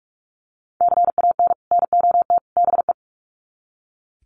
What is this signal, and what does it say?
Morse code, heard as a keyed tone, about 44 wpm, 700 Hz. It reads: CAN NOT BE
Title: Sets of 3 Words 45wpm